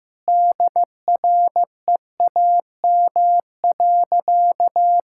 Morse code, heard as a keyed tone, about 16 wpm, 700 Hz. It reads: DREAM.
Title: Sentences from Top 1600 Words 15wpm